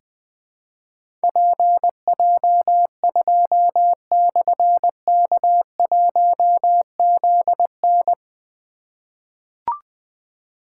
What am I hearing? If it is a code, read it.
Morse code, heard as a keyed tone, about 20 wpm, 700 Hz. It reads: PJ2/K1ZN
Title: Difficult Call Signs 20wpm